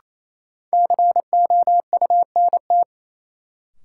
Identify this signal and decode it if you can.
Morse code, heard as a keyed tone, about 28 wpm, 700 Hz. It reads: COUNT
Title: Sentences from Top 1500 Words - New Word Drill Alt - 28wpm